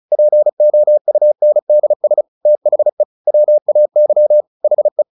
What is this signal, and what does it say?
Morse code, heard as a keyed tone, about 35 wpm, 600 Hz. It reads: POUNDS THE WAY HE
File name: the old man and the sea 2 35wpm